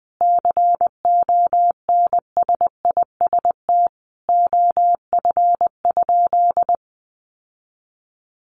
Morse code, heard as a keyed tone, about 20 wpm, 700 Hz. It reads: CONSIST OF?
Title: Sentences from Top 1400 Words 20wpm